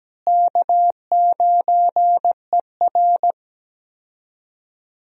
Morse code, heard as a keyed tone, about 17 wpm, 700 Hz. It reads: K9ER
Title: USA Call Signs - 4 Characters - Repeated 3x - 17wpm